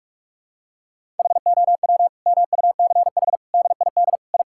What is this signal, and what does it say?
Morse code, heard as a keyed tone, about 46 wpm, 700 Hz. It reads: HOW MUCH DID I
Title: Sentences from Top 1000 Words - Review Part 5 - 45wpm